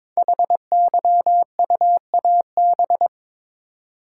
Morse code, heard as a keyed tone, about 22 wpm, 700 Hz. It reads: HYUAB
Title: Code Groups - 5 Letter:Numbers - 22wpm